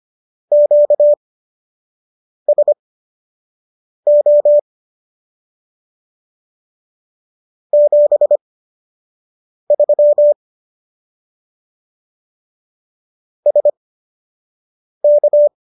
Morse code, heard as a keyed tone, about 25 wpm, spaced as for 6 wpm, 590 Hz. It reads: QSO 73 SK